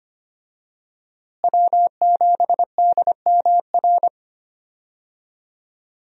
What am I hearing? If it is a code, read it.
Morse code, heard as a keyed tone, about 25 wpm, 700 Hz. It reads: W7DMR
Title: USA Call Signs - 5 Characters - Repeated 3x - 25wpm